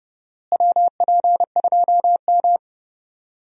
Morse code, heard as a keyed tone, about 30 wpm, 700 Hz. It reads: WP2M
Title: USA Call Signs - 4 Characters - Repeated 3x - 30wpm